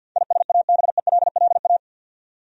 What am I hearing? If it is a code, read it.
Morse code, heard as a keyed tone, about 51 wpm, 700 Hz. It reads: ISABELLA
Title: USA Common Names - 50wpm